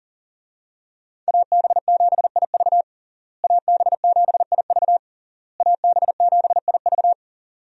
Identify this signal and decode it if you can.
Morse code, heard as a keyed tone, about 40 wpm, 700 Hz. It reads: AB7IV AB7IV AB7IV
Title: USA Call Signs - 5 Characters - Repeated 3x - 40wpm